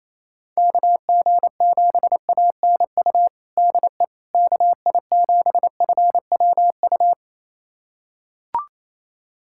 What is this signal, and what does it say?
Morse code, heard as a keyed tone, about 28 wpm, 700 Hz. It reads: KG7ANU DE KI7FWU
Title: USA Callsign DE Callsign - 6 Characters - 28wpm